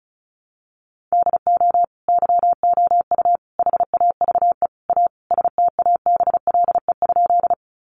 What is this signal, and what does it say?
Morse code, heard as a keyed tone, about 34 wpm, 700 Hz. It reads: DO YOU HAVE A STABLE?
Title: Sentences from Top 1500 Words 35wpm